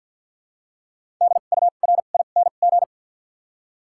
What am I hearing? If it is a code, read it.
Morse code, heard as a keyed tone, about 53 wpm, spaced as for 37 wpm, 700 Hz. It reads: DURING
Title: 2x Character Spacing - Top 500 Words - 50wpm